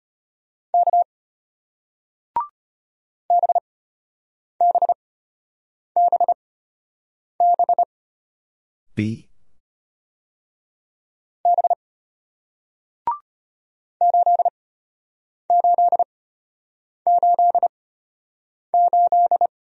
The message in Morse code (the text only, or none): K BBBB B 8888